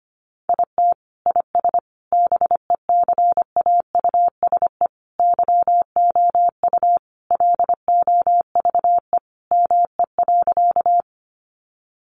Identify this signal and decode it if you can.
Morse code, heard as a keyed tone, about 25 wpm, 700 Hz. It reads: IT IS BECAUSE YOU LOVE ME.
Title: Sentences from Top 600 Words 25wpm